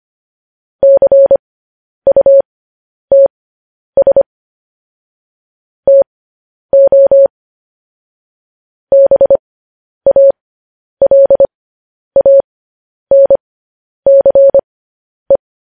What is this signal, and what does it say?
Morse code, heard as a keyed tone, about 25 wpm, spaced as for 10 wpm, 550 Hz. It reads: CUTS TO BALANCE
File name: podcast / News headlines-1x-25x10